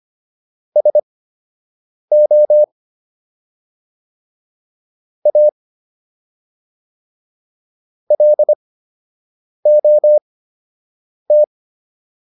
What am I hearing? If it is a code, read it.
Morse code, heard as a keyed tone, about 25 wpm, spaced as for 7 wpm, 600 Hz. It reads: SO A LOT